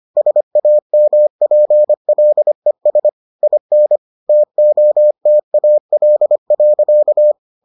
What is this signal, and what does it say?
Morse code, heard as a keyed tone, about 25 wpm, 600 Hz. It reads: SAMPLES IN TOTAL.